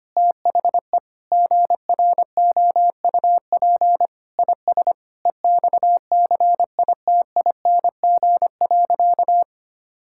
Morse code, heard as a keyed tone, about 25 wpm, 700 Hz. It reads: THE GROUP IS EXCITING.
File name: practice / 25wpm / Sentences with 4-Word Phrases - 25wpm